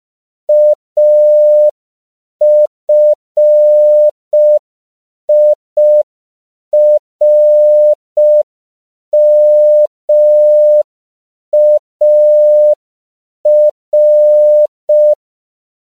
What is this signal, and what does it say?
Morse code, heard as a keyed tone, about 5 wpm, 600 Hz. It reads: AFIRMAR
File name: CW / 5PPM